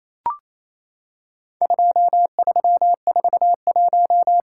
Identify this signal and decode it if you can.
Morse code, heard as a keyed tone, about 28 wpm, 700 Hz. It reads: 2341
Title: Worked All Europe DX Contest - QTC Exchange - 28wpm